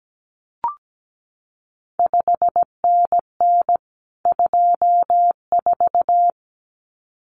Morse code, heard as a keyed tone, about 17 wpm, 700 Hz. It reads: 5NN 24
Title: CQ WW DX Contest Exchange Encore - 17wpm